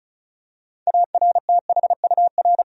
Morse code, heard as a keyed tone, about 34 wpm, 700 Hz. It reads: ARTHUR
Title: USA Common Names - 35wpm